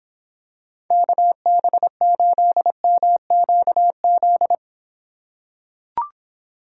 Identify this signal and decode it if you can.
Morse code, heard as a keyed tone, about 26 wpm, 700 Hz. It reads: KB8MQZ